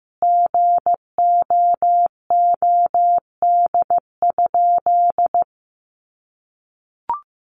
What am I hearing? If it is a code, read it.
Morse code, heard as a keyed tone, about 15 wpm, 700 Hz. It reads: GOOD?
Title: Sentences from Top 1300 Words 15wpm